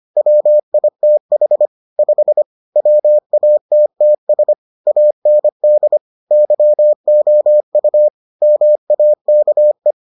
Morse code, heard as a keyed tone, about 25 wpm, 600 Hz. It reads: WITH 5 WATTS AND YOU MAKE